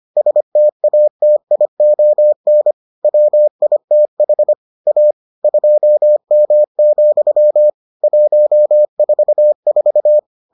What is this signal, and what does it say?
Morse code, heard as a keyed tone, about 25 wpm, 600 Hz. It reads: STATION WITH A 2M, 144